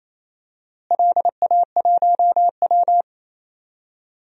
Morse code, heard as a keyed tone, about 28 wpm, 700 Hz. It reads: LA1W
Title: DXCC-100 Call Signs - 4 Characters - Repeated 3x - 28wpm